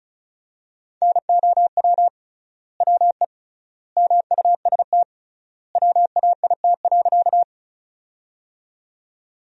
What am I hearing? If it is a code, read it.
Morse code, heard as a keyed tone, about 36 wpm, 700 Hz. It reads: NOW WE MUST WAIT.
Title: 3x Word-Spacing - Sentences from Top 500 Words - 35wpm